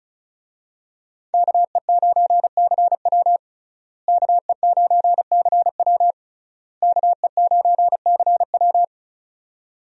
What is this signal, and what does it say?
Morse code, heard as a keyed tone, about 34 wpm, 700 Hz. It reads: KE9CW KE9CW KE9CW
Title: USA Call Signs - 5 Characters - Repeated 3x - 35wpm